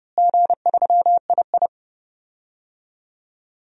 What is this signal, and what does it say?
Morse code, heard as a keyed tone, about 30 wpm, 700 Hz. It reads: G3II
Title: DXCC-100 Call Signs - 4 Characters - Repeated 2x - 30wpm